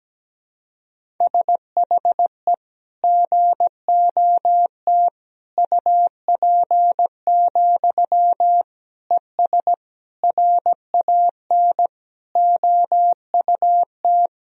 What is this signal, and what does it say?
Morse code, heard as a keyed tone, about 17 wpm, 700 Hz. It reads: SHE GOT UP, ES RAN OUT
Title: Sentences from Top 500 Words and Abbreviations 17wpm